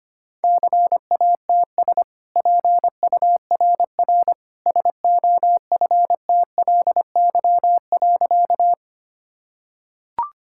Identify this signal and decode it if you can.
Morse code, heard as a keyed tone, about 25 wpm, 700 Hz. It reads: CATS PURR SOFTLY.
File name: Easy Sentences with Hints - 25wpm